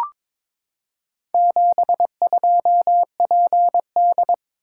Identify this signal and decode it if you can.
Morse code, heard as a keyed tone, about 22 wpm, 690 Hz. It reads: 72PD
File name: practice / 22wpm / Four Letter-Number Combinations - 22wpm